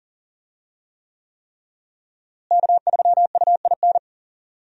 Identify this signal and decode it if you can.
Morse code, heard as a keyed tone, about 40 wpm, 700 Hz. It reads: K3UIN
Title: USA Call Signs - 5 Characters - Repeated 3x - 40wpm